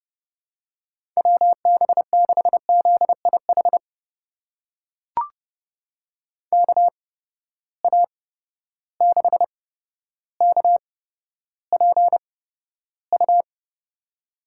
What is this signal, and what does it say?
Morse code, heard as a keyed tone, about 32 wpm, spaced as for 13 wpm, 700 Hz. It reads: WB6ZIH KA6KPU
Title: USA Call Signs - 6 Characters - 8x Character Spacing - 30wpm